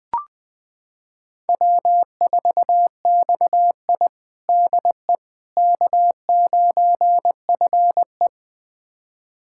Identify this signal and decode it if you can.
Morse code, heard as a keyed tone, about 21 wpm, 700 Hz. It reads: W4XI DE K9FE
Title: USA Callsign DE Callsign - 4 Characters - 20wpm